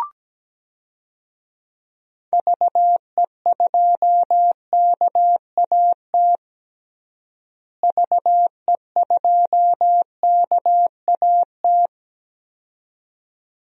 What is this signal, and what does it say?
Morse code, heard as a keyed tone, about 17 wpm, 700 Hz. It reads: VE2KAT VE2KAT
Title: DXCC-100 Call Signs - 6 Characters - Repeated 2x - 17wpm